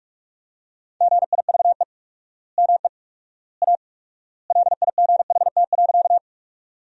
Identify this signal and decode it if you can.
Morse code, heard as a keyed tone, about 44 wpm, 700 Hz. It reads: GIVE ME A LIGHT.
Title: 4x Word-Spacing - Sentences from Top 200 Words - 45wpm